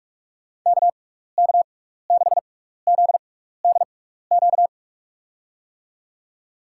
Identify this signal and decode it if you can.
Morse code, heard as a keyed tone, about 46 wpm, spaced as for 15 wpm, 700 Hz. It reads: KK6ZDQ
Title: USA Call Signs - 6 Characters - 6x Character Spacing - 45wpm